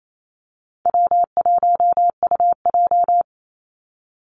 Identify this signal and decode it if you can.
Morse code, heard as a keyed tone, about 28 wpm, 700 Hz. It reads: W1UJ
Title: Call Signs - 28wpm